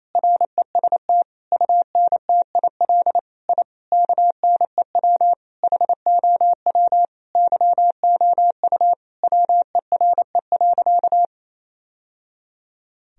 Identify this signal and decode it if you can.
Morse code, heard as a keyed tone, about 28 wpm, 700 Hz. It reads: REST UNTIL I KNEW HOW YOU WERE.
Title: Sentences from Top 1000 Words - Review Part 1 - 28wpm